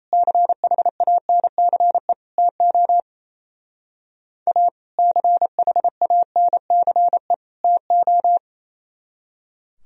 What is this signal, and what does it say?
Morse code, heard as a keyed tone, about 35 wpm, 700 Hz. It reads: CHANCE TO A CHANCE TO